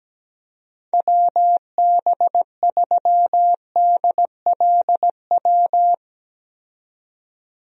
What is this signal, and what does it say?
Morse code, heard as a keyed tone, about 17 wpm, 700 Hz. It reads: WB3DLW
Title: USA Call Signs - 6 Characters - 17wpm